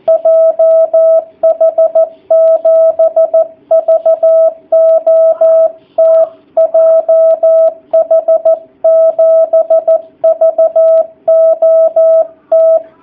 Morse code, heard as a keyed tone, about 14 wpm, 660 Hz. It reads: JH7VOT JH7VOT
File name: jh8cbh / 2019cw / sp13- w